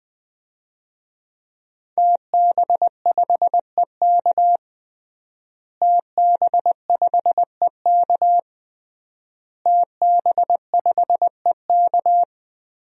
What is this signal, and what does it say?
Morse code, heard as a keyed tone, about 20 wpm, 700 Hz. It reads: TB5EK TB5EK TB5EK
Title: DXCC-100 Call Signs - 5 Characters - Repeated 3x - 20wpm